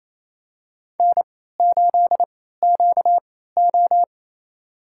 Morse code, heard as a keyed tone, about 27 wpm, spaced as for 16 wpm, 700 Hz. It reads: N8QO
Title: USA Call Signs - 4 Characters - 3x Character Spacing - 28wpm